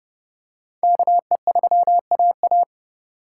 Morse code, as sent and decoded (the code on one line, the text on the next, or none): -.- . ...-- .- .-
KE3AA